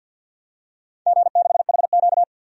Morse code, heard as a keyed tone, about 51 wpm, 700 Hz. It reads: G6HQ
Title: DXCC-100 Call Signs - 4 Characters - 50wpm